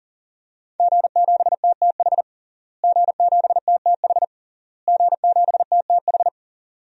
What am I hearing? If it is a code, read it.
Morse code, heard as a keyed tone, about 40 wpm, 700 Hz. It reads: G7TTH G7TTH G7TTH
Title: DXCC-100 Call Signs - 5 Characters - Repeated 3x - 40wpm